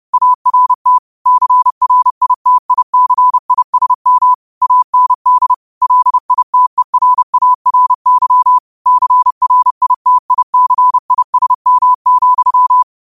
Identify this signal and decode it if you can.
Morse code, heard as a keyed tone, about 30 wpm, 1000 Hz. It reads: ART CRITICISM AND LITERARY CRITICISM,
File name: qotdfiles / 30wpm